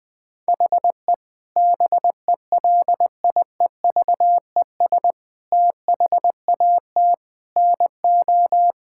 Morse code, heard as a keyed tone, about 20 wpm, 700 Hz. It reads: HE BELIEVES THAT NO
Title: Sentences with 9-Word Phrases - 20wpm